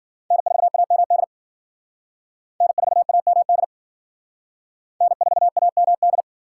Morse code, heard as a keyed tone, about 50 wpm, 690 Hz. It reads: N4AMD N4AMD N4AMD
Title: Speed Racing - USA Call Signs 40wpm